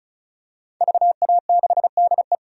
Morse code, heard as a keyed tone, about 35 wpm, 700 Hz. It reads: VA6DE
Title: DXCC-100 Call Signs - 5 Characters - 35wpm